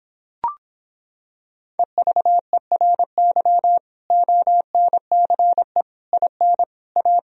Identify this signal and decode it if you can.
Morse code, heard as a keyed tone, about 26 wpm, 700 Hz. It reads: EVERY ONCE IN A